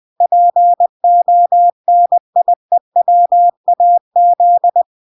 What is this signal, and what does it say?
Morse code, heard as a keyed tone, about 20 wpm, 700 Hz. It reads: PONIEWAZ